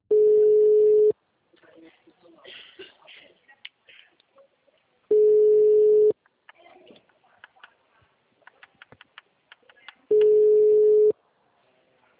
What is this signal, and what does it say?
Ringback tone (Europe, 425 Hz), with speech.